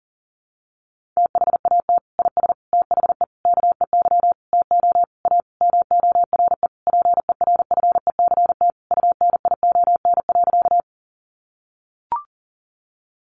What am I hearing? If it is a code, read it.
Morse code, heard as a keyed tone, about 40 wpm, 700 Hz. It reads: THAT IS THE KEY TO A MORE PERFECT UNION.
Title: Sentences from Top 1400 Words 40wpm